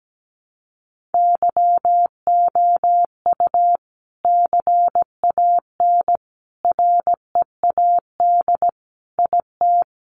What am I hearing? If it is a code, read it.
Morse code, heard as a keyed tone, about 17 wpm, 700 Hz. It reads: YOU CAN READ IT
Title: Sentences from Top 600 Words 17wpm